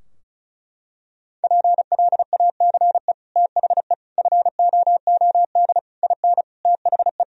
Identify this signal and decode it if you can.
Morse code, heard as a keyed tone, about 36 wpm, 700 Hz. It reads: PLACE THE FOOD IN THE
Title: Sentences from Top 1000 Words - Review Part 4 - 35wpm